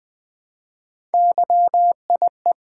Morse code, heard as a keyed tone, about 20 wpm, 700 Hz. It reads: YIE
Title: Build A Word - 1 Letter At A Time - 5-Letter Words - 20wpm